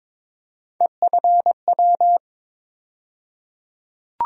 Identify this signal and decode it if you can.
Morse code, heard as a keyed tone, about 21 wpm, 700 Hz. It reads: EFW
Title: USA Call Sign Postfixes - 3 Letters 22wpm